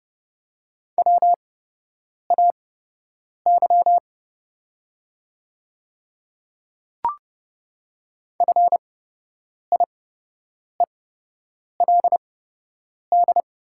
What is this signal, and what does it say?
Morse code, heard as a keyed tone, about 30 wpm, spaced as for 8 wpm, 700 Hz. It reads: WAY FIELD
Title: 8x Character Spacing - Top 500 Words - 30wpm